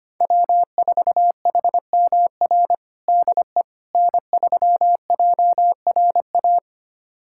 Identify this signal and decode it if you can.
Morse code, heard as a keyed tone, about 25 wpm, 700 Hz. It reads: W4HMR DE N3JRA
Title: USA Callsign DE Callsign - 5 Characters - 25wpm